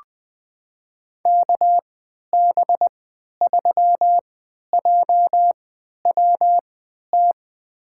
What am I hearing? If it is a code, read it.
Morse code, heard as a keyed tone, about 20 wpm, spaced as for 11 wpm, 700 Hz. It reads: KB3JWT
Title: USA Call Signs - 3x Character Spacing - 20wpm